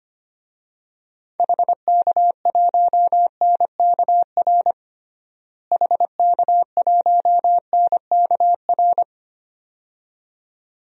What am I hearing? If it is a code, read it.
Morse code, heard as a keyed tone, about 25 wpm, 700 Hz. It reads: HK1NKR HK1NKR
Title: DXCC-100 Call Signs - 6 Characters - Repeated 2x - 25wpm